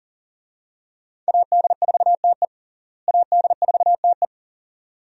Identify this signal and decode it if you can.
Morse code, heard as a keyed tone, about 40 wpm, 700 Hz. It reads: AD4TE AD4TE
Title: USA Call Signs - 5 Characters - Repeated 2x - 40wpm